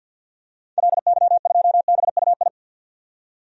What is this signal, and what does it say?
Morse code, heard as a keyed tone, about 51 wpm, 700 Hz. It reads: LY2BUI
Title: DXCC-100 Call Signs - 6 Characters - 50wpm